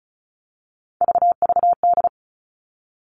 Morse code, heard as a keyed tone, about 36 wpm, 700 Hz. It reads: VVD
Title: Three Challenging Letter-Number Combinations 35wpm